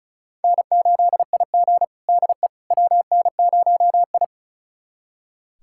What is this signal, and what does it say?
Morse code, heard as a keyed tone, about 34 wpm, 700 Hz. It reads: N8IG DE WN0I
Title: USA Callsign DE Callsign - 4 Characters - 35wpm